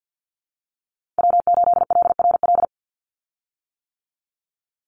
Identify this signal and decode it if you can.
Morse code, heard as a keyed tone, about 50 wpm, 700 Hz. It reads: W8LRL
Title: USA Call Signs 50wpm